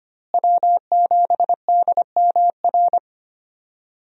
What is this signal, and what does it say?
Morse code, heard as a keyed tone, about 25 wpm, 700 Hz. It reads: W7DMR
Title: USA Call Signs - 5 Characters - Repeated 3x - 25wpm